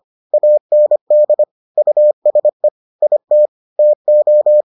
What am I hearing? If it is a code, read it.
Morse code, heard as a keyed tone, about 25 wpm, 600 Hz. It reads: AND USE IT TO